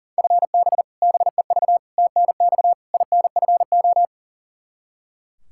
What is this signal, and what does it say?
Morse code, heard as a keyed tone, about 37 wpm, 700 Hz. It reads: FB BEV TNX INFO
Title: Practice QSO - Exchange 3 - More Info - 40wpm